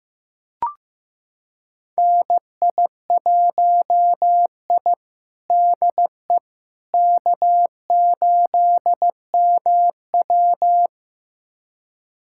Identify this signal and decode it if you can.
Morse code, heard as a keyed tone, about 15 wpm, 700 Hz. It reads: NI1I DE K8MW
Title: USA Callsign DE Callsign - 4 Characters - 15wpm